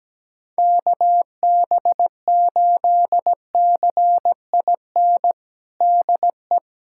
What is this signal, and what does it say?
Morse code, heard as a keyed tone, about 17 wpm, 700 Hz. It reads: KB8CIN DE
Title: USA Callsign DE Callsign - 6 Characters - 17wpm